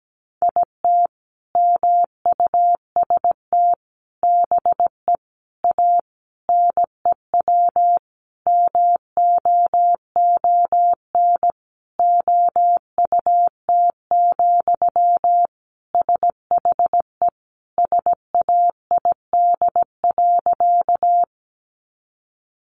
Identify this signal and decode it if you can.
Morse code, heard as a keyed tone, about 17 wpm, 700 Hz. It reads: IT MUST BE A NEW MOON OUT, SHE SAID.
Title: Sentences from Top 500 Words 17wpm